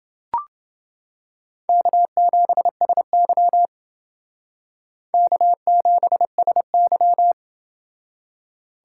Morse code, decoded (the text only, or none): K7SY K7SY